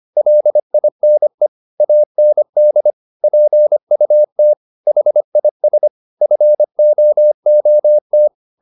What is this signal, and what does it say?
Morse code, heard as a keyed tone, about 25 wpm, 600 Hz. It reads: LINE AND PUT HIS FOOT